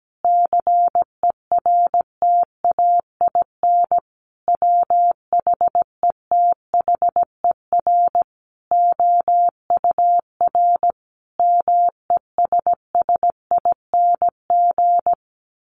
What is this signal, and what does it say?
Morse code, heard as a keyed tone, about 17 wpm, 700 Hz. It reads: CERTAIN WHETHER OUR MESSING
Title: A Sound Of Thunder 17wpm